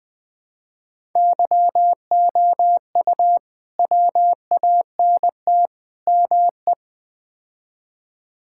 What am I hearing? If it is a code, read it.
Morse code, heard as a keyed tone, about 20 wpm, 700 Hz. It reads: YOU WANT ME
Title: Build A Phrase - 4-Word Phrases - 20wpm